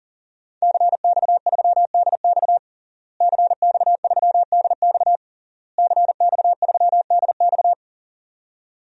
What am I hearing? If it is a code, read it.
Morse code, heard as a keyed tone, about 37 wpm, 700 Hz. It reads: CX3DX CX3DX CX3DX
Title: DXCC-100 Call Signs - 5 Characters - Repeated 3x - 40wpm